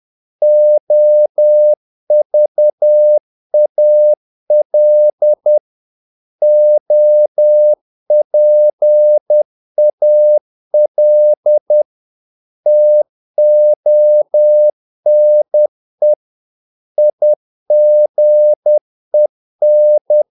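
Morse code, heard as a keyed tone, about 10 wpm, 600 Hz. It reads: OVAL OPAL TONE IGEN
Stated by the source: Morse practice of short Danish words at 10 wpm, from CW med Gnister.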